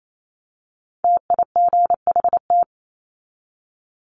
Morse code, heard as a keyed tone, about 28 wpm, 700 Hz. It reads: TIGHT
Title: Top 1751-2000 Words 28wpm